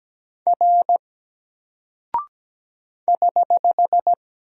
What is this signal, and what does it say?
Morse code, heard as a keyed tone, about 17 wpm, 700 Hz. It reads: R <HH>
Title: Single Character (Letter:Number:Punctuation:Prosign) 17wpm